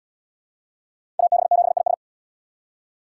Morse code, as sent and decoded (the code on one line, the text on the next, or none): .- -... --... .. ...
AB7IS